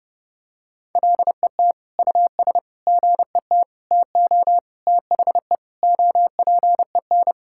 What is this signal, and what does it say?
Morse code, heard as a keyed tone, about 30 wpm, 700 Hz. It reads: LET US GET TO THE OPEN
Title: Sentences from Top 1600 Words 30wpm